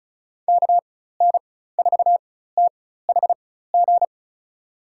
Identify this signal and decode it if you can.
Morse code, heard as a keyed tone, about 36 wpm, spaced as for 16 wpm, 700 Hz. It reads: KN4THG
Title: USA Call Signs - 6 Characters - 4x Character Spacing - 35wpm